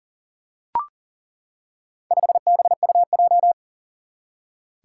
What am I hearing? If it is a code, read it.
Morse code, heard as a keyed tone, about 40 wpm, 700 Hz. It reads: 5BUJ
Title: Four Challenging Letter-Number Combinations 40wpm